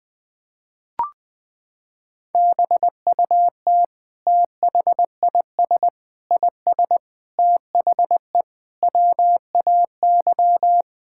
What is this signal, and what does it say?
Morse code, heard as a keyed tone, about 20 wpm, 700 Hz. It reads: BUT THIS IS THE WAY
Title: Sentences from Top 1000 Words - Review Part 5 - 20wpm